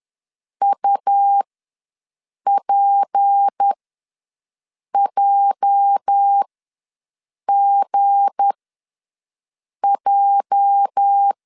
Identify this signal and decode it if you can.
Morse code, heard as a keyed tone, about 10 wpm, spaced as for 5 wpm, 800 Hz. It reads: UPJGJ
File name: LEZIONI / 8 YJ45 / 8 YJ ws 10wpm 4043